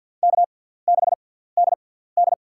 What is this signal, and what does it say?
Morse code, heard as a keyed tone, about 47 wpm, spaced as for 17 wpm, 690 Hz. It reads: K6DD
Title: USA Call Signs - 6x Character Spacing - 50wpm